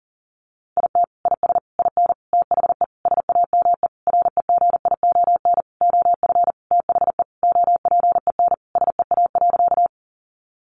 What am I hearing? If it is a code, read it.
Morse code, heard as a keyed tone, about 40 wpm, 700 Hz. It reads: IT IS IN THE SAME REGION OF THE OPEN SEA.
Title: Sentences from Top 1500 Words 40wpm